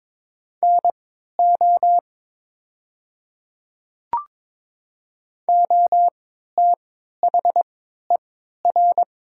Morse code, heard as a keyed tone, about 22 wpm, spaced as for 12 wpm, 700 Hz. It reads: NO OTHER
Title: 3x Character Spacing - Top 100 Words - 22wpm